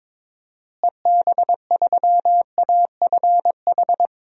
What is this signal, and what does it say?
Morse code, heard as a keyed tone, about 22 wpm, 700 Hz. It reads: EB3AFH
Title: DXCC-100 Call Signs - 6 Characters - Repeated 2x - 22wpm